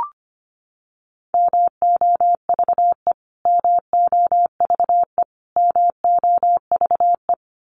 Morse code, heard as a keyed tone, about 25 wpm, 700 Hz. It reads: MOVE MOVE MOVE
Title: Top 300 Words Repeated 25wpm